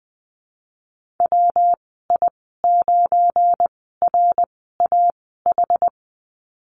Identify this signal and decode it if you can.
Morse code, heard as a keyed tone, about 20 wpm, spaced as for 15 wpm, 700 Hz. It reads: WI9RAH
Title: 173 Lesson 28g - Number 9 - Call Signs 20wpm